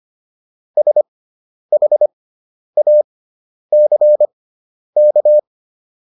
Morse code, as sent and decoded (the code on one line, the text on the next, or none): ... .... .- -.-. -.-
SHACK